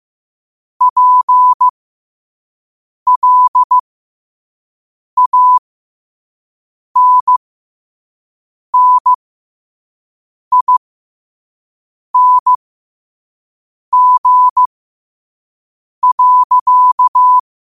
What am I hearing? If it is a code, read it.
Morse code, heard as a keyed tone, about 15 wpm, spaced as for 5 wpm, 1000 Hz. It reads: PLANNING.
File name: qotdfiles / 5wpm